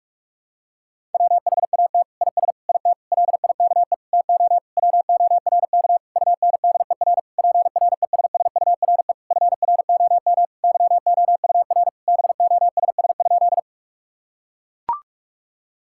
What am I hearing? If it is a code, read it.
Morse code, heard as a keyed tone, about 42 wpm, 700 Hz. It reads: WHAT IS IT LIKE TO WORK UNDER PRESSURE FROM YOUR BOSS?
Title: Sentences from Top 1000 Words - Review Part 6 - 45wpm